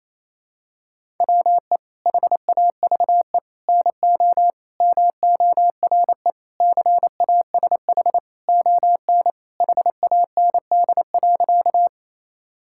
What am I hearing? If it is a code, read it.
Morse code, heard as a keyed tone, about 28 wpm, 700 Hz. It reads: WE HAVE NO MORE CASH ON HAND.
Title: Sentences from Top 1000 Words - Review Part 5 - 28wpm